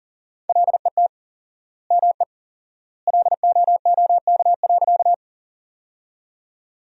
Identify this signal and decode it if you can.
Morse code, heard as a keyed tone, about 40 wpm, 700 Hz. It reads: LET ME LOOK.
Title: 4x Word-Spacing - Sentences from Top 300 Words - 40wpm